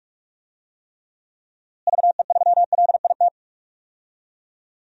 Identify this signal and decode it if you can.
Morse code, heard as a keyed tone, about 46 wpm, 700 Hz. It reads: VE3LIT
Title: DXCC-100 Call Signs - 6 Characters - Repeated 3x - 45wpm